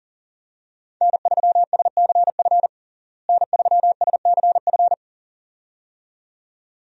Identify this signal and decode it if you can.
Morse code, heard as a keyed tone, about 40 wpm, 700 Hz. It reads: N3SCF N3SCF
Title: USA Call Signs - 5 Characters - Repeated 2x - 40wpm